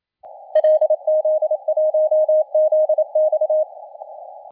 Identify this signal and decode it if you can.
Morse code, heard as a keyed tone, about 27 wpm, 620 Hz. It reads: LZ1ZX
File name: la4c / sac-cw-2017 / CONTEST AUDIO 1100Z